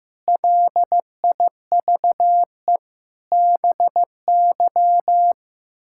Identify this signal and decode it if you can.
Morse code, heard as a keyed tone, about 15 wpm, 700 Hz. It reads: LIVE BY
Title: Sentences from Top 1600 Words 15wpm